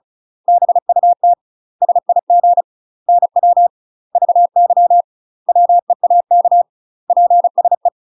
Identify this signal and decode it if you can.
Morse code, heard as a keyed tone, about 34 wpm, 700 Hz. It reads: BUT SIG NW VY WEAK PSE
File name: UK MASTER QSOs TRAINING / 35WPM / UK MASTER QSOs 0001